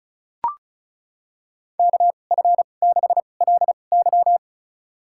Morse code, heard as a keyed tone, about 36 wpm, spaced as for 25 wpm, 700 Hz. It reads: KF6LY
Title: USA Call Signs - 5 Characters - 2x Character Spacing - 35wpm